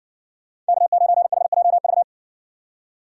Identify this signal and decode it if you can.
Morse code, heard as a keyed tone, about 55 wpm, 700 Hz. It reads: K9HJV